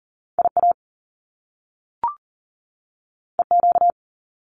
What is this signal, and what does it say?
Morse code, heard as a keyed tone, about 38 wpm, 700 Hz. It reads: IA EQ
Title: Top Two Letter Combinations in Words 40wpm